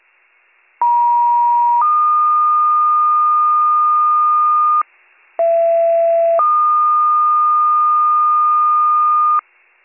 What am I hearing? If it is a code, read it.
Two-tone page, heard as a one-second tone, then a three-second tone, about 950 then 1220 Hz, about 670 then 1220 Hz. It is heard from 0.8 to 9.4 s.